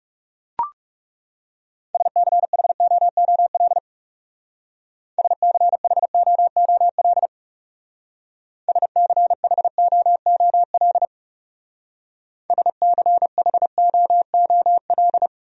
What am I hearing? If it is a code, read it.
Morse code, heard as a keyed tone, about 36 wpm, 700 Hz. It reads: SCHOOL SCHOOL SCHOOL SCHOOL
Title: Speed-Racing Top 500 Words 30wpm